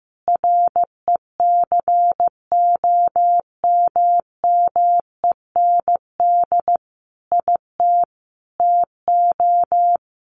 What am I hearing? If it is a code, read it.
Morse code, heard as a keyed tone, about 15 wpm, 700 Hz. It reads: RECOMMEND IT TO
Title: Sentences from Top 1200 Words 15wpm